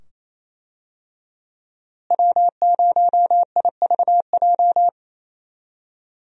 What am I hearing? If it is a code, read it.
Morse code, heard as a keyed tone, about 28 wpm, 700 Hz. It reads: W0IVJ
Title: USA Call Signs - 5 Characters - Repeated 2x - 28wpm